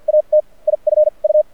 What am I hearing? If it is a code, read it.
Morse code, heard as a keyed tone, about 48 wpm, spaced as for 38 wpm, 610 Hz. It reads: AT IF U
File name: archived / CRSnet